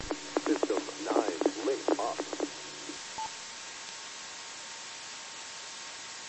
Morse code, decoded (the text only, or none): none